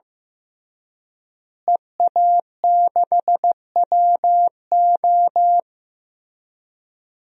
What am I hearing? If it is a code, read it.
Morse code, heard as a keyed tone, about 15 wpm, 700 Hz. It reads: EA6WO
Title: DXCC-100 Call Signs - 5 Characters - Repeated 3x - 15wpm